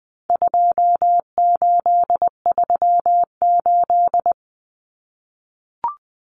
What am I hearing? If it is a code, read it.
Morse code, heard as a keyed tone, about 20 wpm, 700 Hz. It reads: 2838
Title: Common Cut Numbers - 4-Digit 20wpm